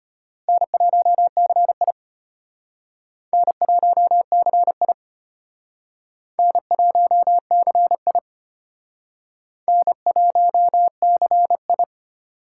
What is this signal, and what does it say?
Morse code, heard as a keyed tone, about 32 wpm, 700 Hz. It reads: N1CI N1CI N1CI N1CI